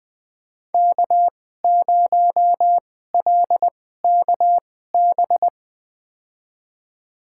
Morse code, heard as a keyed tone, about 21 wpm, spaced as for 15 wpm, 700 Hz. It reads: K0LKB
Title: USA Call Signs - 5 Characters - 2x Character Spacing - 20wpm